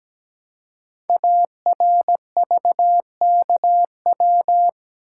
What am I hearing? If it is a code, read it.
Morse code, heard as a keyed tone, about 17 wpm, 700 Hz. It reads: ARVKW
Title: Code Groups - 5 Letters - 17wpm